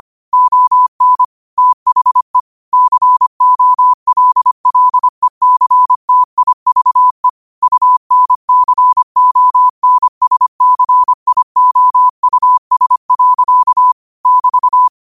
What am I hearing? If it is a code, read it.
Morse code, heard as a keyed tone, about 25 wpm, 1000 Hz. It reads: ON THE COLLECTIVE UNCONSCIOUS. <BT>